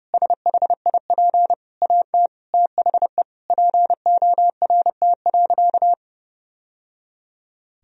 Morse code, heard as a keyed tone, about 30 wpm, 700 Hz. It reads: SHIP AT THE PORT.
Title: Sentences from Top 1000 Words - Review Part 2 - 30wpm